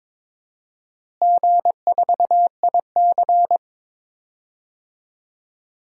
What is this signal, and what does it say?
Morse code, heard as a keyed tone, about 22 wpm, 700 Hz. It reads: G4IC
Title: DXCC-100 Call Signs - 4 Characters - Repeated 2x - 22wpm